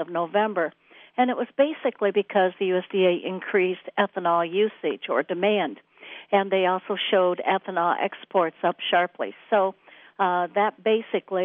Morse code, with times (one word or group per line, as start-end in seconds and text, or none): none